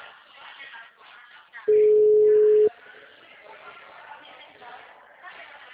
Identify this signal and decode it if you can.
Ringback tone (Europe, 425 Hz), with speech.